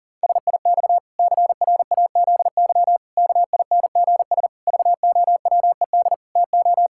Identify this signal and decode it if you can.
Morse code, heard as a keyed tone, about 37 wpm, 700 Hz. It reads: SIX CRAZY KINGS VOWED TO
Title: Pangram Challenge v2 - 40wpm